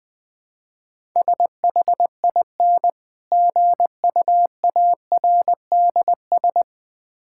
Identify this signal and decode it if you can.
Morse code, heard as a keyed tone, about 20 wpm, 700 Hz. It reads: SHIN GUARDS
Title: Difficult Words - Speed-Racing with Hints - 20wpm